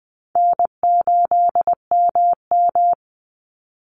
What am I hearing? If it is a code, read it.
Morse code, heard as a keyed tone, about 20 wpm, 700 Hz. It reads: N8MM
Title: USA Call Signs 20wpm